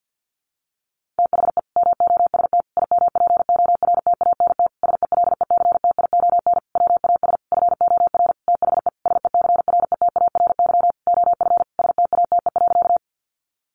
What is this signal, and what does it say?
Morse code, heard as a keyed tone, about 50 wpm, 700 Hz. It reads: THE MOST IMPORTANT SELECTION WAS FOR THE SECRETARY OF STATE.
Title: Sentences from Top 1400 Words 50wpm